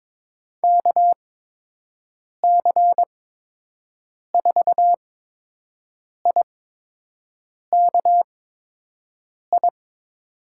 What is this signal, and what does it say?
Morse code, heard as a keyed tone, about 22 wpm, spaced as for 6 wpm, 700 Hz. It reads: KC4IKI
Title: USA Call Signs - 8x Character Spacing - 22wpm